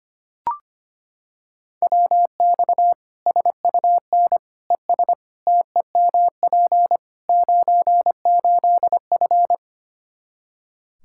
Morse code, heard as a keyed tone, about 25 wpm, 700 Hz. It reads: WX SUN ES TEMP 98F
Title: K1USN SST QSO Practice 25wpm